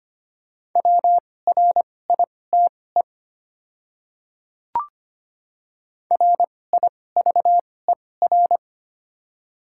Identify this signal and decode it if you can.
Morse code, heard as a keyed tone, about 25 wpm, spaced as for 18 wpm, 690 Hz. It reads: WRITE RIVER
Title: 2x Character Spacing - Top 300 Words - 25wpm